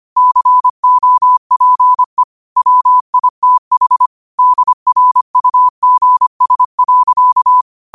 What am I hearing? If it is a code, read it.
Morse code, heard as a keyed tone, about 25 wpm, 1000 Hz. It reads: COPE WITH DRUGS.
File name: qotdfiles / 25wpm